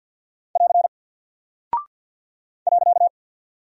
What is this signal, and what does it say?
Morse code, heard as a keyed tone, about 48 wpm, 700 Hz. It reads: <AR> .
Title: Single Character (Letter:Number:Punctuation:Prosign) Alt 50wpm